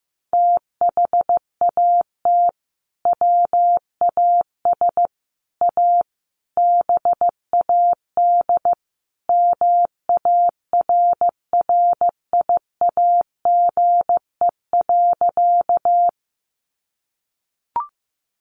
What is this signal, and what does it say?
Morse code, heard as a keyed tone, about 15 wpm, 700 Hz. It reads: THAT WAS A BAD MARRIAGE.
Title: Sentences from Top 800 Words 15wpm